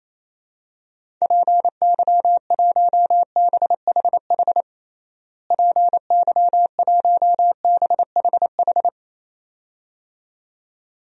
Morse code, heard as a keyed tone, about 28 wpm, 700 Hz. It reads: PY1BHH PY1BHH
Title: DXCC-100 Call Signs - 6 Characters - Repeated 2x - 28wpm